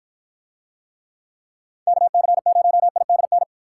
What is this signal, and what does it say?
Morse code, heard as a keyed tone, about 54 wpm, 700 Hz. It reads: KC0IDN